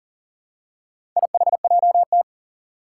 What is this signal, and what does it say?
Morse code, heard as a keyed tone, about 43 wpm, 700 Hz. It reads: IHJT
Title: Four Challenging Letter-Number Combinations 40wpm